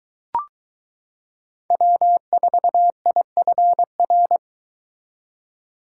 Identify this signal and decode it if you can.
Morse code, heard as a keyed tone, about 22 wpm, 700 Hz. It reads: W4IFR